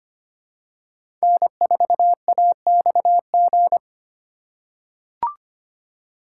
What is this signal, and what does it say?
Morse code, heard as a keyed tone, about 25 wpm, 700 Hz. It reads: N4AXG
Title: USA Call Signs - 5 Characters - 2x Character Spacing - 25wpm